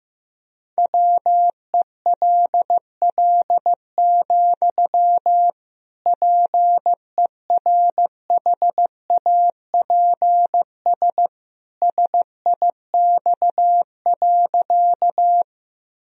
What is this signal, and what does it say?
Morse code, heard as a keyed tone, about 15 wpm, 700 Hz. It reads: WELL, PERHAPS SIX.
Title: Sentences from Top 1000 Words - Review Part 2 - 15wpm